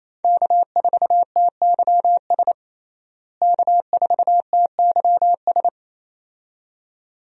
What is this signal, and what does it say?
Morse code, heard as a keyed tone, about 28 wpm, 700 Hz. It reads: K4TYS K4TYS
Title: USA Call Signs - 5 Characters - Repeated 2x - 28wpm